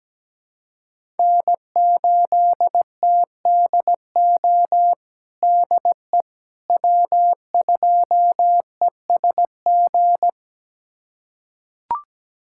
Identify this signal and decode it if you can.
Morse code, heard as a keyed tone, about 17 wpm, 700 Hz. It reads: N8TDO DE W2ESG
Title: USA Callsign DE Callsign - 5 Characters - 17wpm